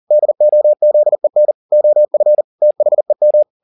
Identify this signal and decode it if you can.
Morse code, heard as a keyed tone, about 40 wpm, 600 Hz. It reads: DOZEN OF THEM
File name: the old man and the sea 2 40wpm